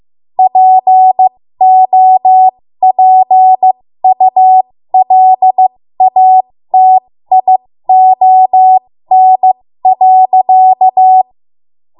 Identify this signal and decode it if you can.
Morse code, heard as a keyed tone, about 15 wpm, spaced as for 13 wpm, 750 Hz. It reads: POPULATION.
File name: Morse / 13WPM